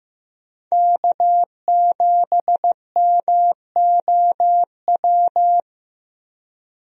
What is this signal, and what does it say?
Morse code, heard as a keyed tone, about 15 wpm, 700 Hz. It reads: K7MOW
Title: USA Call Signs - 5 Characters - Repeated 2x - 15wpm